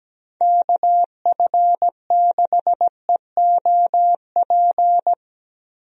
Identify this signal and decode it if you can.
Morse code, heard as a keyed tone, about 17 wpm, 700 Hz. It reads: KF6EOP
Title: USA Call Signs - 6 Characters - Repeated 3x - 17wpm